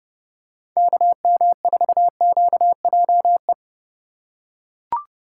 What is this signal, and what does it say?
Morse code, heard as a keyed tone, about 30 wpm, 690 Hz. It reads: KM4QJE
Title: USA Call Signs - 3x Character Spacing - 30wpm